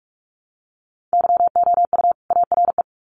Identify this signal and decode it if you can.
Morse code, heard as a keyed tone, about 44 wpm, 700 Hz. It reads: YOU ARE
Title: Sets of 2 Words 45wpm